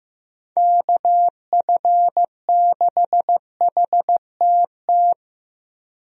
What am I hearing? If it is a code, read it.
Morse code, heard as a keyed tone, about 15 wpm, 700 Hz. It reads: KF6HTT
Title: USA Call Signs - 6 Characters - Repeated 3x - 15wpm